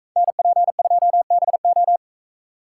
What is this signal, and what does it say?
Morse code, heard as a keyed tone, about 41 wpm, 700 Hz. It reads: NP2BO